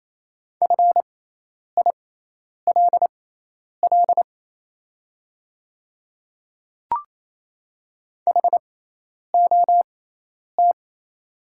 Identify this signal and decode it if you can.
Morse code, heard as a keyed tone, about 27 wpm, spaced as for 10 wpm, 700 Hz. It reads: FILL HOT
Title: 6x Character Spacing - Top 500 Words - 28wpm